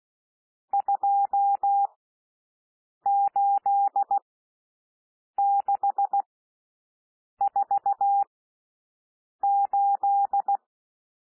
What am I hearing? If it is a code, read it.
Morse code, heard as a keyed tone, about 16 wpm, spaced as for 6 wpm, 800 Hz. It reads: 28648
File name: SOUND NUMBERS 16 wpm